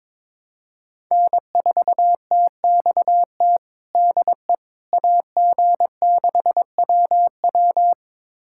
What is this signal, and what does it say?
Morse code, heard as a keyed tone, about 22 wpm, 700 Hz. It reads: N4TXT DE AG6WW
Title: USA Callsign DE Callsign - 5 Characters - 22wpm